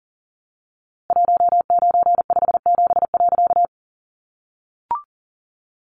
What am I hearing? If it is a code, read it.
Morse code, heard as a keyed tone, about 40 wpm, 700 Hz. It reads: 1957.
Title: A Sound Of Thunder 40wpm